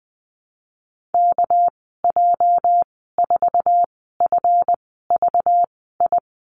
Morse code, heard as a keyed tone, about 20 wpm, spaced as for 15 wpm, 700 Hz. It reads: KJ4FVI
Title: 213 Lesson 34g - Number 8 - Call Signs 20wpm